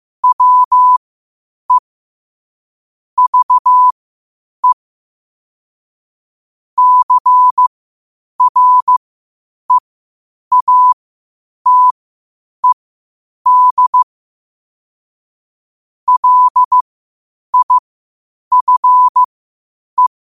Morse code, heard as a keyed tone, about 15 wpm, spaced as for 8 wpm, 1000 Hz. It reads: WE VE CREATED LIFE
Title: 8 WPM morse code quotes for Fri, 01 Mar 2024